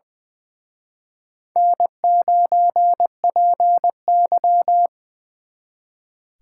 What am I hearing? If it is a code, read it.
Morse code, heard as a keyed tone, about 24 wpm, 700 Hz. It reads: N9PY
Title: Speed Racing - USA 1x2 Call Signs 20wpm